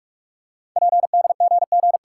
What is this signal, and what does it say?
Morse code, heard as a keyed tone, about 44 wpm, 690 Hz. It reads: PDGG
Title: Code Groups - 5 Letters - 45wpm